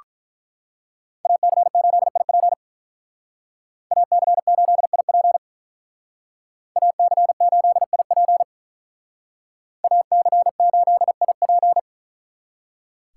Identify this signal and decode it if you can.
Morse code, heard as a keyed tone, about 43 wpm, 700 Hz. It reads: AC8IP AC8IP AC8IP AC8IP
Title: Speed Racing - USA 2x2 Call Signs 35wpm